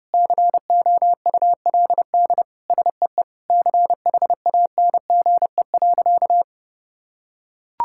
Morse code, heard as a keyed tone, about 30 wpm, 700 Hz. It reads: COULD SEE CHANGE.
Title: Sentences from Top 1000 Words - Review Part 5 - 30wpm